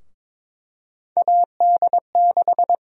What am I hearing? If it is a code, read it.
Morse code, heard as a keyed tone, about 21 wpm, 700 Hz. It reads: AD6
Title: USA Call Sign Prefixes - 22wpm